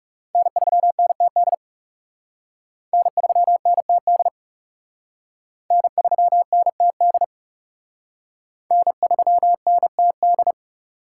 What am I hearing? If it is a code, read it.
Morse code, heard as a keyed tone, about 36 wpm, 700 Hz. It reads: N3NTD N3NTD N3NTD N3NTD
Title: Speed Racing - USA 1x3 Call Signs 30wpm